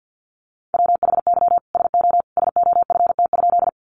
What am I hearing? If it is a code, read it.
Morse code, heard as a keyed tone, about 50 wpm, 700 Hz. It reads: WHY SO SOFT?
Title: Sentences from Top 1500 Words 50wpm